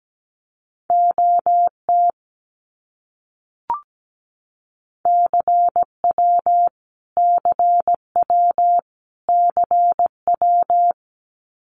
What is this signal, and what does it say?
Morse code, heard as a keyed tone, about 17 wpm, 700 Hz. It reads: OT CW CW CW
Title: Abbreviations Repeated 3x - 17wpm